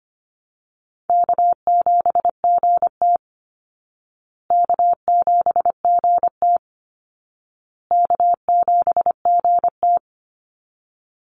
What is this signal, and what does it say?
Morse code, heard as a keyed tone, about 25 wpm, 700 Hz. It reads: K7GT K7GT K7GT
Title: Easy Call Signs Repeated 3x - 25wpm